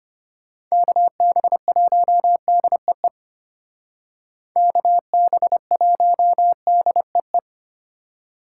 Morse code, heard as a keyed tone, about 33 wpm, 700 Hz. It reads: KB1DEE KB1DEE